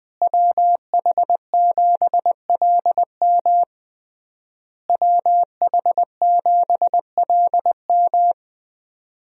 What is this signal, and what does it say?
Morse code, heard as a keyed tone, about 21 wpm, 700 Hz. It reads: WH7LM WH7LM
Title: USA Call Signs - 5 Characters - Repeated 3x - 20wpm